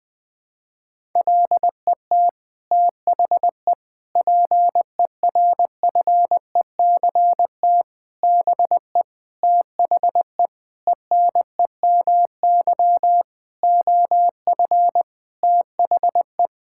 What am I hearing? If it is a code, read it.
Morse code, heard as a keyed tone, about 20 wpm, 700 Hz. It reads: LET THE PERFECT BE THE ENEMY OF THE
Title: Sets of 9 Words - 20wpm